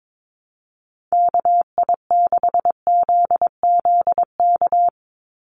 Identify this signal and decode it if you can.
Morse code, heard as a keyed tone, about 22 wpm, 700 Hz. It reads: KI6ZZK
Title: USA Call Signs 22wpm